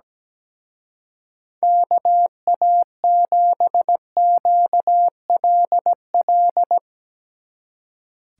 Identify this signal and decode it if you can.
Morse code, heard as a keyed tone, about 19 wpm, 700 Hz. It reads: KA7QLL
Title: Speed Racing - USA 2x3 Call Signs 17wpm